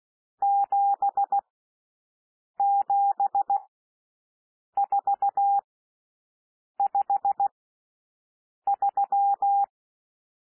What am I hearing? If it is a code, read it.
Morse code, heard as a keyed tone, about 16 wpm, spaced as for 6 wpm, 800 Hz. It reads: 77453
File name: SOUND NUMBERS 16 wpm